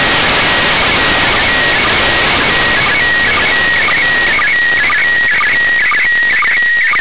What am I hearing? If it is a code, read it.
SSTV (slow-scan television), PD 120.
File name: commit / satnogs 29407 short